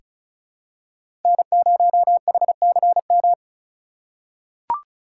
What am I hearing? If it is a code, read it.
Morse code, heard as a keyed tone, about 36 wpm, 700 Hz. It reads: N0HCM
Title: USA Call Signs - 5 Characters - 3x Character Spacing - 35wpm